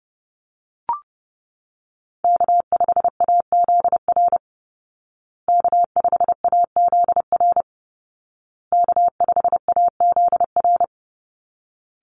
Morse code, heard as a keyed tone, about 30 wpm, 700 Hz. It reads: K5AZR K5AZR K5AZR
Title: Call Signs Repeated 2x - 30wpm